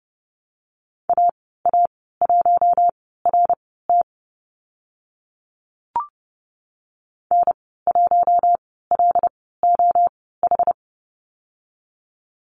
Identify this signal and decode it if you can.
Morse code, heard as a keyed tone, about 30 wpm, spaced as for 17 wpm, 700 Hz. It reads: AA1RT N1LOH
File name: Lesson 15g - Number 5 - Call Signs 30wpm